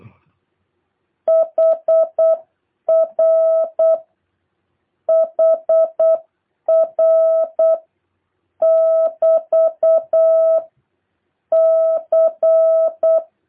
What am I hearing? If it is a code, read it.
Morse code, heard as a keyed tone, about 8 wpm, 660 Hz. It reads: HR HR <BT> C